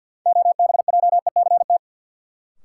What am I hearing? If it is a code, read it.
Morse code, heard as a keyed tone, about 50 wpm, 700 Hz. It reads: OBJECT
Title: Sentences from Top 500 Words - New Word Drill - 50wpm